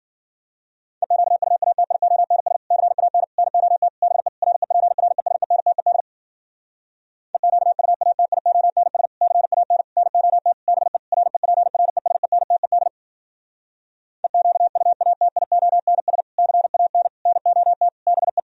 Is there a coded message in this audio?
Yes: Morse code from 1.0 to 18.4 s, heard as a keyed tone, about 59 wpm, 700 Hz. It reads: EQUATIONS CAN NOT BE REPRESENTED EQUATIONS CAN NOT BE REPRESENTED EQUATIONS CAN NOT BE